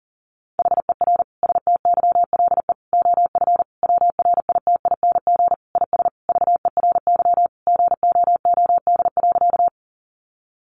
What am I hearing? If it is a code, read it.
Morse code, heard as a keyed tone, about 40 wpm, 700 Hz. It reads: HER STYLE OF WRITING IS VERY GOOD.
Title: Sentences from Top 800 Words 40wpm